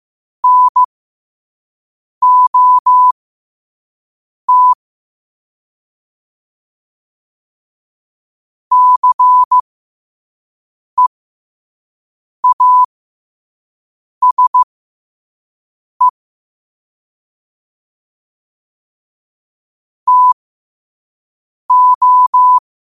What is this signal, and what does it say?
Morse code, heard as a keyed tone, about 15 wpm, spaced as for 5 wpm, 1000 Hz. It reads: NOT CEASE TO
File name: qotdfiles / 5wpm